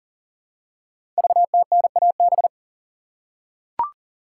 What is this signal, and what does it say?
Morse code, heard as a keyed tone, about 38 wpm, 690 Hz. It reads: VTNAB
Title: Code Groups - 5 Letters - 40wpm